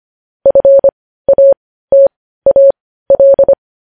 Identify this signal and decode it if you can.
Morse code, heard as a keyed tone, about 25 wpm, spaced as for 15 wpm, 550 Hz. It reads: FATAL
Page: News Headlines in Morse Code at 15 WPM